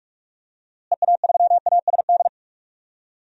Morse code, heard as a keyed tone, about 46 wpm, 700 Hz. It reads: EA3ASD
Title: DXCC-100 Call Signs - 6 Characters - Repeated 3x - 45wpm